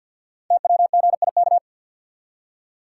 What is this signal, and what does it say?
Morse code, heard as a keyed tone, about 48 wpm, 700 Hz. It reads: TWGIK
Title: Code Groups - 5 Letters - 50wpm